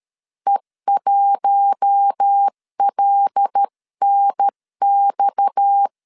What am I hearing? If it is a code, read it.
Morse code, heard as a keyed tone, about 13 wpm, 800 Hz. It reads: E1LNX
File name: CW / DET / 1 6 1500C S C 12wpm 3533